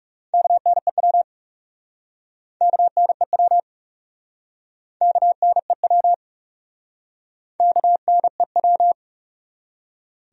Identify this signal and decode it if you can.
Morse code, heard as a keyed tone, about 42 wpm, 700 Hz. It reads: KNEW KNEW KNEW KNEW